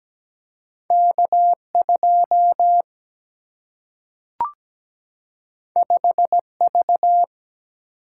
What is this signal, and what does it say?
Morse code, heard as a keyed tone, about 17 wpm, 700 Hz. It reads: K2 5V
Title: Single-Letter Single-Number 17wpm